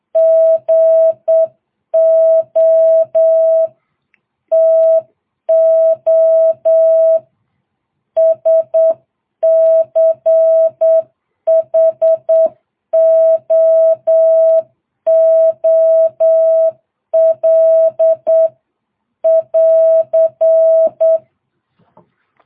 Morse code, heard as a keyed tone, about 8 wpm, 660 Hz. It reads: GO TO SCHOOL <AR>